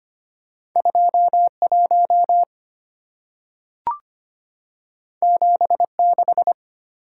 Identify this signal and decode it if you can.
Morse code, heard as a keyed tone, about 25 wpm, 700 Hz. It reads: 21 76
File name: Two Letter-Number Combinations - 25wpm